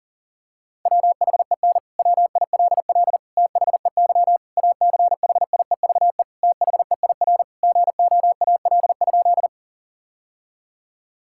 Morse code, heard as a keyed tone, about 40 wpm, 700 Hz. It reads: WHEN WILL THEY ACHIEVE THEIR GOAL?
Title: Sentences from Top 1000 Words - Review Part 6 - 40wpm